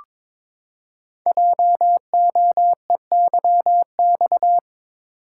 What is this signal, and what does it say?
Morse code, heard as a keyed tone, about 22 wpm, 700 Hz. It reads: JOEYX
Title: Code Groups - 5 Letter:Numbers - 22wpm